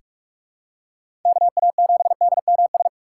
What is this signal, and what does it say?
Morse code, heard as a keyed tone, about 46 wpm, 700 Hz. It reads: KA7DMS
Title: USA Call Signs - 6 Characters - 4x Character Spacing - 45wpm